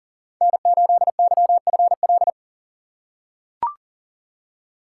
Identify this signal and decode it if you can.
Morse code, heard as a keyed tone, about 37 wpm, 700 Hz. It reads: N8YFL
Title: USA Call Signs - 5 Characters - 3x Character Spacing - 40wpm